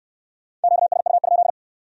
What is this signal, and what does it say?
Morse code, heard as a keyed tone, about 56 wpm, 700 Hz. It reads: QSU?